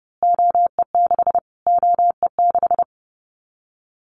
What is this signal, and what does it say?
Morse code, heard as a keyed tone, about 30 wpm, 700 Hz. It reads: OE6 OE6
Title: Callsign Prefixes for Top 25 DXCC - Repeated 3x - 30wpm